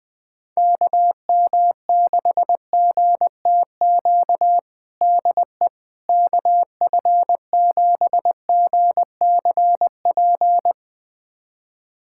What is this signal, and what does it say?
Morse code, heard as a keyed tone, about 20 wpm, 700 Hz. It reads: KM6GTQ DE KF7GCP
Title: USA Callsign DE Callsign - 6 Characters - 20wpm